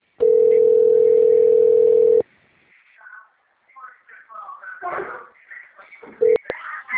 Ringback tone (North America), 440 + 480 Hz.